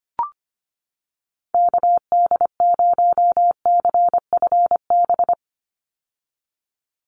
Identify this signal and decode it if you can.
Morse code, heard as a keyed tone, about 25 wpm, 700 Hz. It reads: KD0CFB
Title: USA Call Signs 25wpm